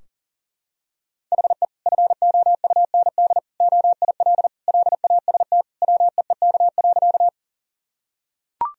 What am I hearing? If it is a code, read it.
Morse code, heard as a keyed tone, about 40 wpm, 700 Hz. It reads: HE FOUND OIL LAST WEEK.
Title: Sentences from Top 1000 Words - Review Part 4 - 40wpm